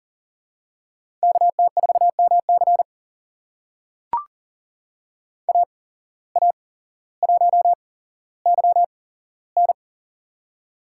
Morse code, heard as a keyed tone, about 39 wpm, spaced as for 17 wpm, 700 Hz. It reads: KT4MC AA1YN
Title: USA Call Signs - 5 Characters - 8x Character Spacing - 40wpm